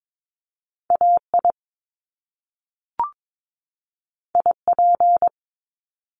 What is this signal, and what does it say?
Morse code, heard as a keyed tone, about 22 wpm, 700 Hz. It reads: AI IP
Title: Top Two Letter Combinations in Words 22wpm